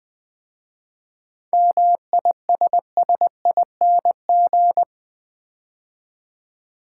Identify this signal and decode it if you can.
Morse code, heard as a keyed tone, about 20 wpm, 690 Hz. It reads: MISSING
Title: 7 Letter Words - First 3 Letters Given - 20wpm